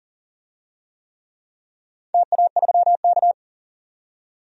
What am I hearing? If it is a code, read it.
Morse code, heard as a keyed tone, about 40 wpm, 700 Hz. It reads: TA3K
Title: DXCC-100 Call Signs - 4 Characters - 40wpm